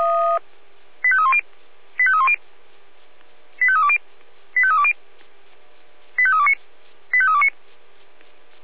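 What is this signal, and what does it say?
Five-tone selective call (ZVEI), heard as quick runs of five tones, tones of about 70 ms. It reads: 85310 85310 85320 85320 85329 85329
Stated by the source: ZVEI five-tone selective call 85898 received on an SDR.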